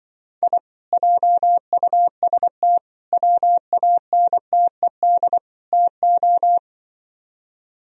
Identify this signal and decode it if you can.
Morse code, heard as a keyed tone, about 24 wpm, 700 Hz. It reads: I JUST WANTED TO
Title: Speed-Racing - Sets of 4 Words 20wpm